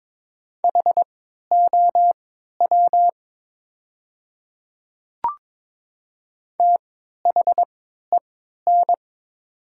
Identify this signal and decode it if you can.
Morse code, heard as a keyed tone, about 22 wpm, spaced as for 13 wpm, 700 Hz. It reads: HOW THEN
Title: 3x Character Spacing - Top 100 Words - 22wpm